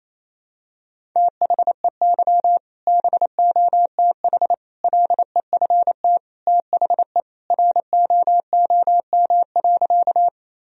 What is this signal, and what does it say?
Morse code, heard as a keyed tone, about 27 wpm, 700 Hz. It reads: THEY BOTH LEFT THE ROOM.
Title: Sentences from Top 1000 Words - Review Part 1 - 28wpm